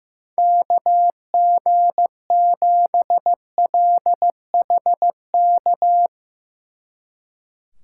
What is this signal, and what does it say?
Morse code, heard as a keyed tone, about 15 wpm, 700 Hz. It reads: KG7LHK
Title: Speed Racing - USA Call Signs 15wpm